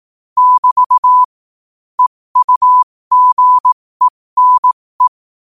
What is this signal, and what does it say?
Morse code, heard as a keyed tone, about 18 wpm, spaced as for 15 wpm, 1000 Hz. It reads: <BT> EUGENE
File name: qotdfiles / 15wpm